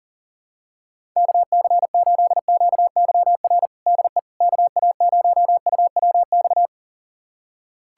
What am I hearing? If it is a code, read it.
Morse code, heard as a keyed tone, about 40 wpm, 700 Hz. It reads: KC8QYR DE KA0UWX
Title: USA Callsign DE Callsign - 6 Characters - 40wpm